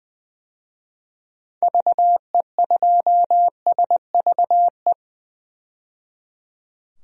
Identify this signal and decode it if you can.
Morse code, heard as a keyed tone, about 20 wpm, 700 Hz. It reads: VE2SVE
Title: DXCC-100 Call Signs - 6 Characters - 20wpm